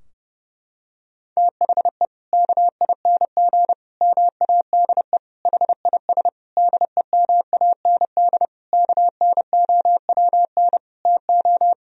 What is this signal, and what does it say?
Morse code, heard as a keyed tone, about 32 wpm, 700 Hz. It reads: THE KING MADE HIS DEMAND KNOWN TO
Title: Sentences from Top 1000 Words - Review Part 4 - 30wpm